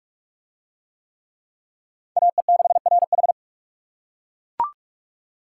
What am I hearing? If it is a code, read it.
Morse code, heard as a keyed tone, about 42 wpm, 700 Hz. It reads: AE6RH